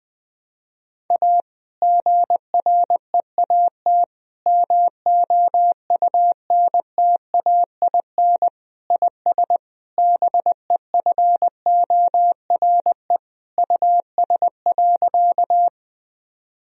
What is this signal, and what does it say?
Morse code, heard as a keyed tone, about 20 wpm, 700 Hz. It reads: A GREAT MOUNTAIN IS BEFORE US.
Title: Sentences from Top 1000 Words - Review Part 1 - 20wpm